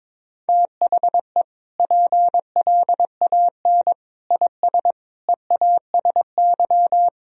Morse code, heard as a keyed tone, about 22 wpm, 700 Hz. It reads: THE PLAN IS EASY
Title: Sentences from Top 1000 Words - Review Part 4 - 22wpm